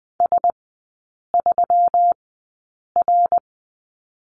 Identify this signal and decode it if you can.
Morse code, heard as a keyed tone, about 20 wpm, spaced as for 8 wpm, 700 Hz. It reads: S3R
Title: 121 Lesson 20e - Number 3 - Three Characters 20wpm